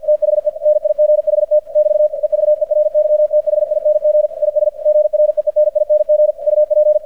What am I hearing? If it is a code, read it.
Morse code, heard as a keyed tone, about 50 wpm, 600 Hz. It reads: THE NIGHT <BT> EVERYTHING IS GREENING UP
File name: archived / CRSnet